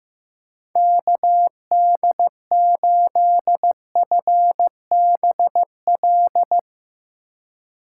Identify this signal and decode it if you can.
Morse code, heard as a keyed tone, about 15 wpm, 700 Hz. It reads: KD8FBL
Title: USA Call Signs - 6 Characters - 4x Character Spacing - 15wpm